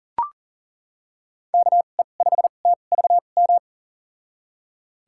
Morse code, heard as a keyed tone, about 40 wpm, spaced as for 29 wpm, 700 Hz. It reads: KE5TVM
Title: USA Call Signs - 6 Characters - 2x Character Spacing - 40wpm